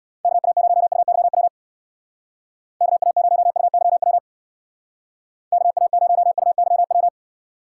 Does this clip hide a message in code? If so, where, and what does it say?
Yes: Morse code from 0.2 to 7.1 s, heard as a keyed tone, about 65 wpm, 700 Hz. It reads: KA0UYW KA0UYW KA0UYW